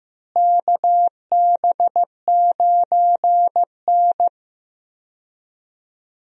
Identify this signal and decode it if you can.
Morse code, heard as a keyed tone, about 15 wpm, 700 Hz. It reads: KB9N
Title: USA Call Signs - 4 Characters - Repeated 2x - 15wpm